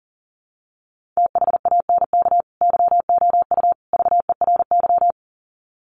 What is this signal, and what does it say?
Morse code, heard as a keyed tone, about 40 wpm, 700 Hz. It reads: THANK YOU VERY
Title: Sets of 3 Words 40wpm